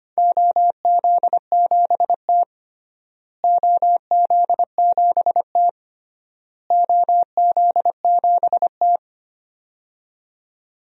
Morse code, heard as a keyed tone, about 25 wpm, 700 Hz. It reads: OZ7T OZ7T OZ7T
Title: DXCC-100 Call Signs - 4 Characters - Repeated 3x - 25wpm